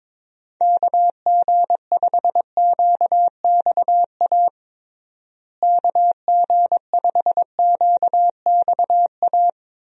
Morse code, heard as a keyed tone, about 23 wpm, 700 Hz. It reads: KG5QXA KG5QXA
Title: USA Call Signs - 6 Characters - Repeated 3x - 22wpm